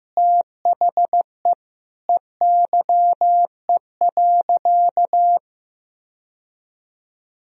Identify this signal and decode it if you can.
Morse code, heard as a keyed tone, about 15 wpm, 700 Hz. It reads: THE EYE.
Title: Sentences from Top 1000 Words - Review Part 1 - 15wpm